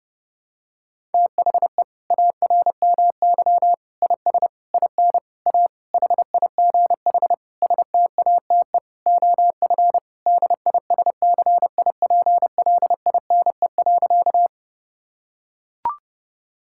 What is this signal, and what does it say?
Morse code, heard as a keyed tone, about 32 wpm, 700 Hz. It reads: THE ARMY IS IN A HIGH STATE OF DISCIPLINE.
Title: Sentences from Top 1000 Words - Review Part 6 - 30wpm